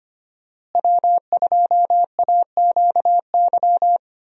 Morse code, heard as a keyed tone, about 25 wpm, 700 Hz. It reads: W2AQY
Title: USA Call Signs - 5 Characters - Repeated 3x - 25wpm